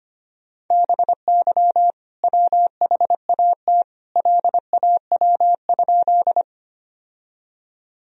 Morse code, heard as a keyed tone, about 25 wpm, 700 Hz. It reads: BY WHAT LAW?
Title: Sentences from Top 1000 Words - Review Part 3 - 25wpm